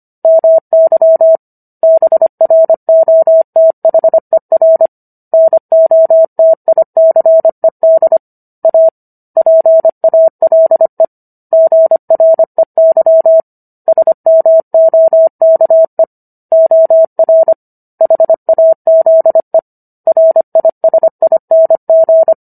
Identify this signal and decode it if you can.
Morse code, heard as a keyed tone, about 25 wpm, 650 Hz. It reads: MY BROTHER NOTICED A PALE GREY SMOKE OR HAZE RISING